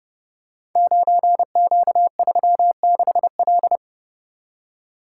Morse code, heard as a keyed tone, about 29 wpm, 700 Hz. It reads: 9Q36L
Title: Code Groups - 5 Letter:Numbers - 30wpm